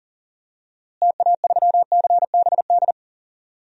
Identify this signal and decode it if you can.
Morse code, heard as a keyed tone, about 40 wpm, 700 Hz. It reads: TA3CBD
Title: DXCC-100 Call Signs - 6 Characters - Repeated 3x - 40wpm